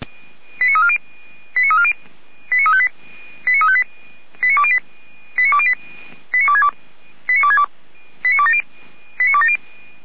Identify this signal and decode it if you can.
Five-tone selective call (ZVEI), heard as quick runs of five tones, tones of about 70 ms. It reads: 89340 89340 89358 89358 89208 89262 89262 89270 89270